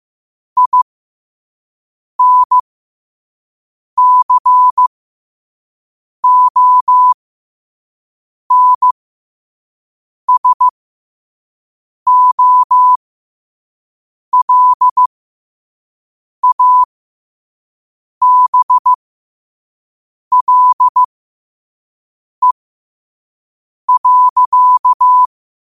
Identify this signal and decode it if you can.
Morse code, heard as a keyed tone, about 15 wpm, spaced as for 5 wpm, 1000 Hz. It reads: INCONSOLABLE.
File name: qotdfiles / 5wpm